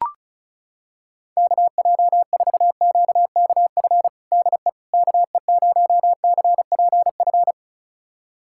Morse code, heard as a keyed tone, about 36 wpm, 700 Hz. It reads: KJ4QKF DE KE0CPF
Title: USA Callsign DE Callsign - 6 Characters - 35wpm